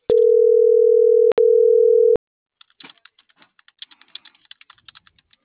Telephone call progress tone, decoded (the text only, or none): ringback tone